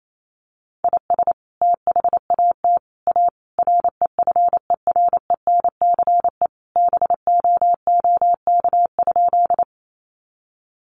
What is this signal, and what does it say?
Morse code, heard as a keyed tone, about 28 wpm, 700 Hz. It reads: IS THAT A REFERENCE BOOK?
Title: Sentences from Top 1200 Words 28wpm